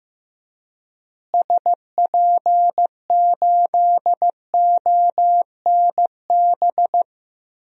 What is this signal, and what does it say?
Morse code, heard as a keyed tone, about 15 wpm, 700 Hz. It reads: SP8ONB
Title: DXCC-100 Call Signs - 6 Characters - Repeated 3x - 15wpm